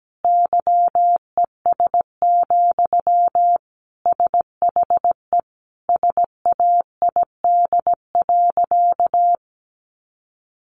Morse code, heard as a keyed tone, about 17 wpm, 700 Hz. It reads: YES, SHE SAID.
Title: Sentences from Top 1300 Words 17wpm